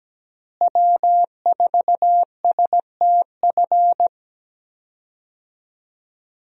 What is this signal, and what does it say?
Morse code, heard as a keyed tone, about 17 wpm, 700 Hz. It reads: W4STF
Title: USA Call Signs - 5 Characters - Repeated 2x - 17wpm